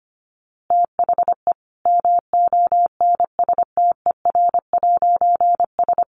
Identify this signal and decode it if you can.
Morse code, heard as a keyed tone, about 25 wpm, 700 Hz. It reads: THE MONSTER'S
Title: A Sound Of Thunder 25wpm